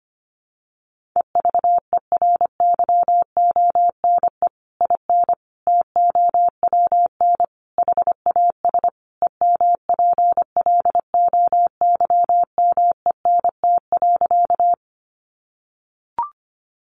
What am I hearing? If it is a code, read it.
Morse code, heard as a keyed tone, about 25 wpm, 700 Hz. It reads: EVERYONE IN TOWN HAS EMPLOYMENT.
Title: Sentences from Top 1500 Words 25wpm